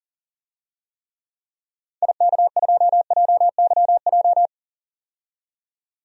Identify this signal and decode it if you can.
Morse code, heard as a keyed tone, about 40 wpm, 700 Hz. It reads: IK2JYJ
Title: DXCC-100 Call Signs - 6 Characters - 40wpm